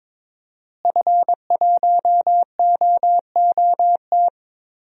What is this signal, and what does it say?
Morse code, heard as a keyed tone, about 21 wpm, 700 Hz. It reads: F1OOT
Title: DXCC-100 Call Signs - 5 Characters - Repeated 2x - 22wpm